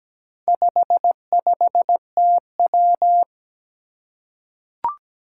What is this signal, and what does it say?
Morse code, heard as a keyed tone, about 17 wpm, 700 Hz. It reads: 55TW
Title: Four Challenging Letter-Number Combinations 17wpm